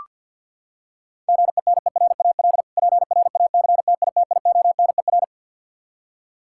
Morse code, heard as a keyed tone, about 46 wpm, 700 Hz. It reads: GENERAL PRACTITIONER
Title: K1USN SST QSO Practice 50wpm